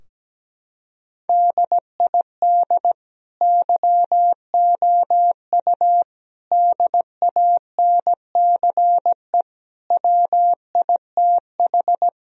Morse code, heard as a keyed tone, about 17 wpm, 700 Hz. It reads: DID YOU DANCE WITH
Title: Sentences from Top 1000 Words - Review Part 4 - 17wpm